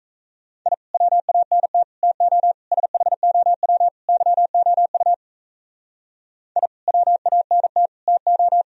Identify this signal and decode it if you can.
Morse code, heard as a keyed tone, about 34 wpm, 700 Hz. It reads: I WANT TO SHOW YOU I WANT TO
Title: Speed-Racing - Sets of 5 Words 28wpm